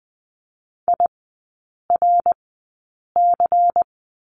Morse code, heard as a keyed tone, about 20 wpm, spaced as for 8 wpm, 700 Hz. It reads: IRC
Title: 145 Lesson 24e - Letter y - Three Characters 20wpm